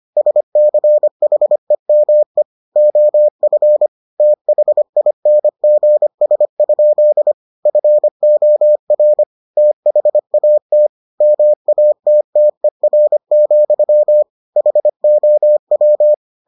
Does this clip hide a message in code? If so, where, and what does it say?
Yes: Morse code from 0.2 to 16.1 s, heard as a keyed tone, about 25 wpm, 600 Hz. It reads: SCHEME OF THINGS? FOR THAT MATTER, HOW